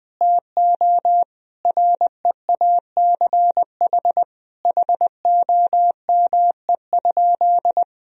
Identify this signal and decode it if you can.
Morse code, heard as a keyed tone, about 20 wpm, 700 Hz. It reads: TO REACH HOME?
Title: Sentences from Top 1000 Words - Review Part 2 - 20wpm